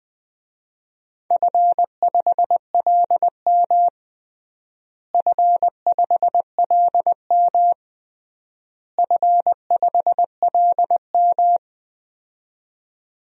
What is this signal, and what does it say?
Morse code, heard as a keyed tone, about 20 wpm, 700 Hz. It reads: F5LM F5LM F5LM
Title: DXCC-100 Call Signs - 4 Characters - Repeated 3x - 20wpm